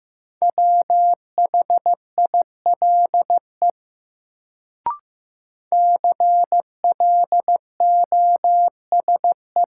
Morse code, heard as a keyed tone, about 15 wpm, 700 Hz. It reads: WHILE CLOSE
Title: Sentences from Top 300 Words - New Word Drill - 15wpm